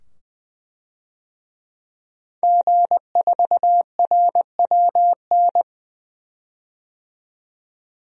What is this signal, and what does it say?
Morse code, heard as a keyed tone, about 21 wpm, 700 Hz. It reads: G4RWN
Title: DXCC-100 Call Signs - 5 Characters - 20wpm